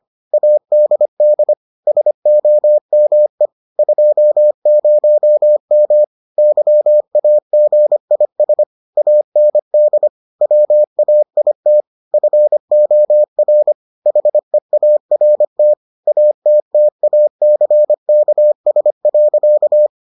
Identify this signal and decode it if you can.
Morse code, heard as a keyed tone, about 25 wpm, 600 Hz. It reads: ADD SOME 20M YAGIS AND WAIT FOR HEART ATTACKS.